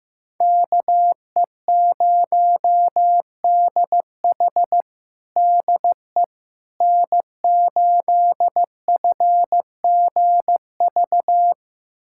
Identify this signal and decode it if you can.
Morse code, heard as a keyed tone, about 16 wpm, 700 Hz. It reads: KE0DH DE N8FGV
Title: USA Callsign DE Callsign - 5 Characters - 15wpm